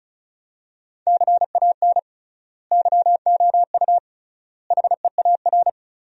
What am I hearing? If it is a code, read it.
Morse code, heard as a keyed tone, about 36 wpm, 700 Hz. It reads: CAN YOU HEAR
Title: 3x Word-Spacing - Sentences from Top 300 Words - 35wpm